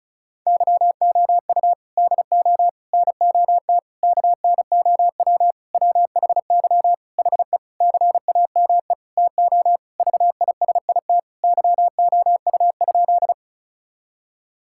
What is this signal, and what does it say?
Morse code, heard as a keyed tone, about 36 wpm, 700 Hz. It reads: YOU DO NOT KNOW WHY HE CAME TO VISIT YOU?
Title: Sentences from Top 1000 Words - Review Part 6 - 35wpm